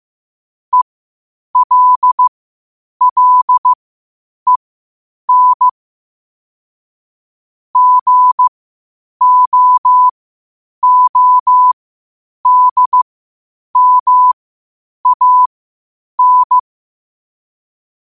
Morse code, heard as a keyed tone, about 15 wpm, spaced as for 8 wpm, 1000 Hz. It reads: ELLEN GOODMAN
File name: qotdfiles / 8wpm